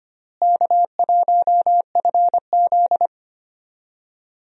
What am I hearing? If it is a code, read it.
Morse code, heard as a keyed tone, about 25 wpm, 700 Hz. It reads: K1FZ
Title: USA Call Signs - 4 Characters - Repeated 3x - 25wpm